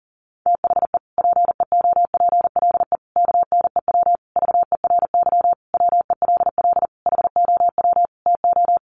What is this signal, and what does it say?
Morse code, heard as a keyed tone, about 40 wpm, 700 Hz. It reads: THE PEOPLE KNEW VERY WELL HOW TO
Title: Sentences from Top 1100 Words 40wpm